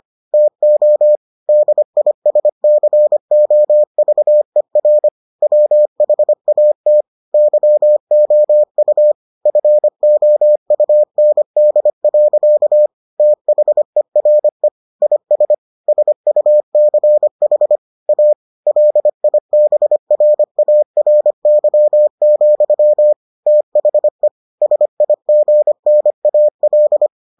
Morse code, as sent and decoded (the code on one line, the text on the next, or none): - --- / -.. .. ... -.-. --- ...- . .-. / .-- .... .- - / -.-- --- ..- / ..-. --- ..- -. -.. .-.-.- / - .... .-. / .. ... / ... ..- -.-. .... / .- / .-.. .. -... .-. .- .-. -.-- --..-- / - .... / ... .. --. -. .- .-..
TO DISCOVER WHAT YOU FOUND. THERE IS SUCH A LIBRARY, THE SIGNAL